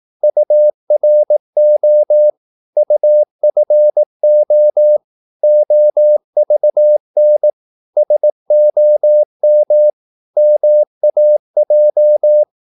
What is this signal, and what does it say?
Morse code, heard as a keyed tone, about 18 wpm, 600 Hz. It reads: URO UFO OVN SOM MAJ
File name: Korte ord DK 18wpm